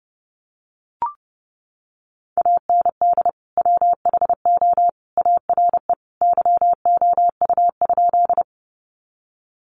Morse code, heard as a keyed tone, about 30 wpm, 700 Hz. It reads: AND WHO ARE YOU?
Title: Sentences from Top 100 Words 30wpm